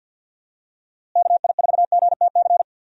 Morse code, heard as a keyed tone, about 50 wpm, 700 Hz. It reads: KI4GTC
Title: USA Call Signs - 6 Characters - Repeated 2x - 50wpm